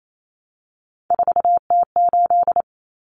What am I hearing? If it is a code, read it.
Morse code, heard as a keyed tone, about 28 wpm, 700 Hz. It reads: 4T8
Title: Common Cut Numbers - 3-Digit 28wpm